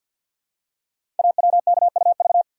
Morse code, heard as a keyed tone, about 46 wpm, 700 Hz. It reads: AWKUV
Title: Code Groups - 5 Letters - 50wpm